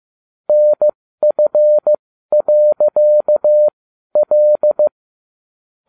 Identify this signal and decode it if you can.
Morse code, heard as a keyed tone, about 15 wpm, spaced as for 12 wpm, 600 Hz. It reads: NF.L